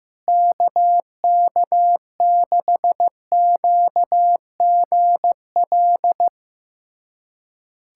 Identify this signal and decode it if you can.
Morse code, heard as a keyed tone, about 15 wpm, 700 Hz. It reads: KK6QGL
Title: USA Callsign DE Callsign 15wpm